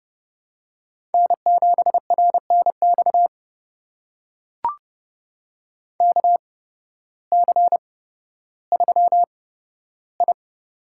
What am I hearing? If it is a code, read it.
Morse code, heard as a keyed tone, about 30 wpm, spaced as for 14 wpm, 700 Hz. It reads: N7RNX KC3I
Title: USA Call Signs - 8x Character Spacing - 30wpm